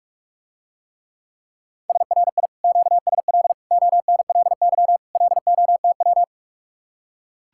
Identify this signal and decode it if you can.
Morse code, heard as a keyed tone, about 44 wpm, 700 Hz. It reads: SRI QSL ONLY LOTW